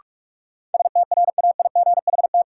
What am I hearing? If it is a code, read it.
Morse code, heard as a keyed tone, about 46 wpm, 700 Hz. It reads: STRAIGHT
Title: Sentences from Top 1000 Words - New Word Drill - 45wpm